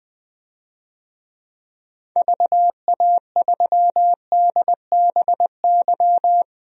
Morse code, heard as a keyed tone, about 20 wpm, 700 Hz. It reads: VA3DBY
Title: DXCC-100 Call Signs - 6 Characters - 20wpm